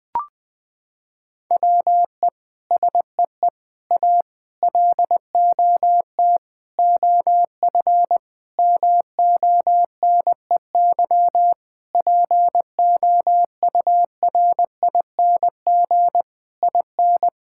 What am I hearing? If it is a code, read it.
Morse code, heard as a keyed tone, about 21 wpm, 700 Hz. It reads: WE SEE A LOT OF MONEY POURING IN
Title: Sets of 9 Words - 20wpm